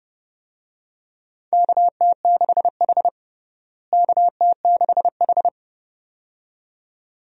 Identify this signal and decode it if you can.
Morse code, heard as a keyed tone, about 29 wpm, 700 Hz. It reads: KT6H KT6H
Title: USA Call Signs - 4 Characters - Repeated 2x - 30wpm